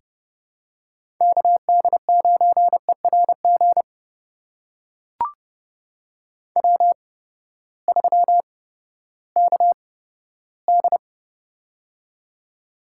Morse code, heard as a keyed tone, about 30 wpm, spaced as for 15 wpm, 700 Hz. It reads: KD9ERG W3KD
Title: USA Call Signs - 8x Character Spacing - 30wpm